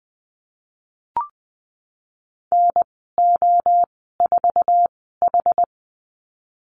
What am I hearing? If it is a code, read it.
Morse code, heard as a keyed tone, about 20 wpm, spaced as for 15 wpm, 700 Hz. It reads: NO4H
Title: 062 Lesson 11g - Letter h - Call Signs 20wpm